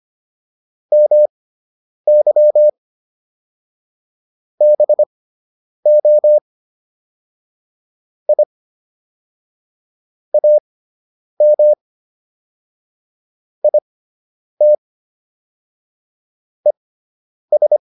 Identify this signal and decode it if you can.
Morse code, heard as a keyed tone, about 25 wpm, spaced as for 9 wpm, 600 Hz. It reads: MY BO I AM IT ES